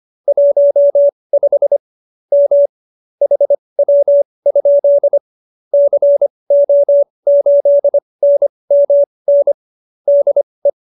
Morse code, heard as a keyed tone, about 25 wpm, spaced as for 20 wpm, 570 Hz. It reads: 15 M HW? CO8NMN DE